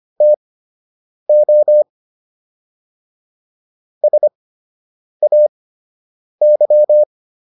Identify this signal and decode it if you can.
Morse code, heard as a keyed tone, about 25 wpm, spaced as for 8 wpm, 600 Hz. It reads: TO SAY